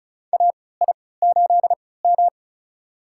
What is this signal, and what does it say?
Morse code, heard as a keyed tone, about 36 wpm, spaced as for 20 wpm, 700 Hz. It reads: AI8M
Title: USA Call Signs - 3x Character Spacing - 35wpm